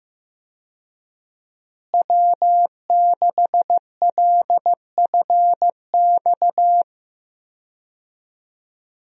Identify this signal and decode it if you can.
Morse code, heard as a keyed tone, about 15 wpm, 700 Hz. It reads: W6LFX
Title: USA Call Signs - 5 Characters - Repeated 2x - 15wpm